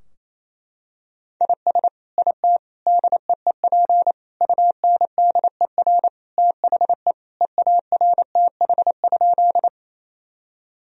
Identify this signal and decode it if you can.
Morse code, heard as a keyed tone, about 28 wpm, 700 Hz. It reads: IS IT DEEP UNDER THE EARTH?
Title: Sentences from Top 1000 Words - Review Part 3 - 28wpm